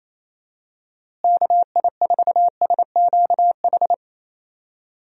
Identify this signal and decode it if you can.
Morse code, heard as a keyed tone, about 28 wpm, 700 Hz. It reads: KI4SQH
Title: USA Call Signs - 6 Characters - Repeated 3x - 28wpm